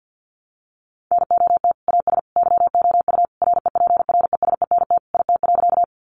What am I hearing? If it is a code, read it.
Morse code, heard as a keyed tone, about 50 wpm, 700 Hz. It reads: NOT AS YOU REPRESENT IT.
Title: Sentences from Top 1200 Words 50wpm